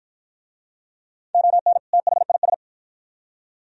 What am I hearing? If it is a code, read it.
Morse code, heard as a keyed tone, about 51 wpm, 700 Hz. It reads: ON THIS